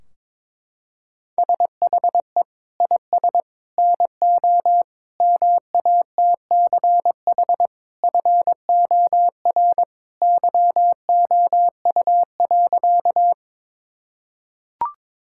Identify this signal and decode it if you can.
Morse code, heard as a keyed tone, about 22 wpm, 700 Hz. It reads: SHE IS NO MATCH FOR YOU.
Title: Sentences from Top 1600 Words 22wpm